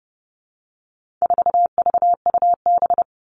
Morse code, heard as a keyed tone, about 30 wpm, 700 Hz. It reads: 4VUB
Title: All Cut Numbers - 4-Digit 30wpm